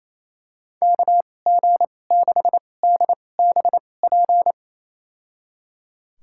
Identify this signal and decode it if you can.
Morse code, heard as a keyed tone, about 28 wpm, spaced as for 20 wpm, 700 Hz. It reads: KG6DBP
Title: USA Call Signs - 2x Character Spacing - 28wpm